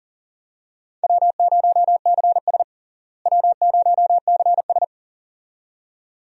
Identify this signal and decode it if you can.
Morse code, heard as a keyed tone, about 40 wpm, 700 Hz. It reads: W0CS W0CS
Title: USA Call Signs - 4 Characters - Repeated 2x - 40wpm